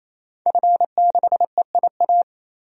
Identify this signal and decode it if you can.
Morse code, heard as a keyed tone, about 27 wpm, 700 Hz. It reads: F6EIA
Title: DXCC-100 Call Signs - 5 Characters - 28wpm